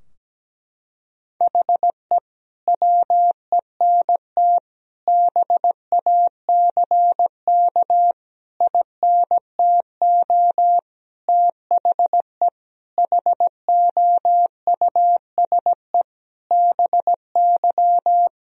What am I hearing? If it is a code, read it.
Morse code, heard as a keyed tone, about 17 wpm, 700 Hz. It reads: HE WENT BACK INTO THE HOUSE BY
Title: Sentences from Top 1000 Words - Review Part 2 - 17wpm